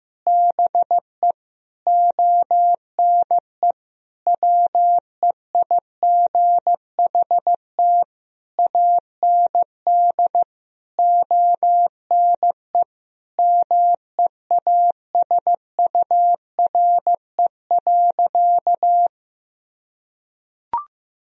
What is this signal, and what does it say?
Morse code, heard as a keyed tone, about 15 wpm, 700 Hz. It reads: BE ONE WEIGHT AND ONE MEASURE.
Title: Sentences from Top 1000 Words - Review Part 3 - 15wpm